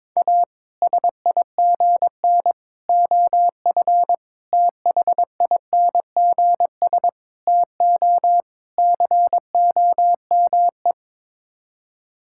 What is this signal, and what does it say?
Morse code, heard as a keyed tone, about 22 wpm, 700 Hz. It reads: A SIGN OF THINGS TO COME
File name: practice / 22wpm / Sets of 6 Words - 22wpm